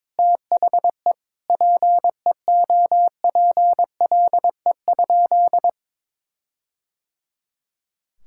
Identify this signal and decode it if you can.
Morse code, heard as a keyed tone, about 22 wpm, 700 Hz. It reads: THE PEOPLE?
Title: Sentences from Top 1600 Words 22wpm